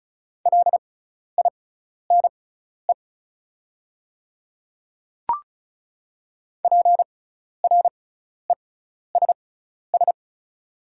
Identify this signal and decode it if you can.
Morse code, heard as a keyed tone, about 34 wpm, spaced as for 12 wpm, 700 Hz. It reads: LINE PRESS
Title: 6x Character Spacing - Top 300 Words - 35wpm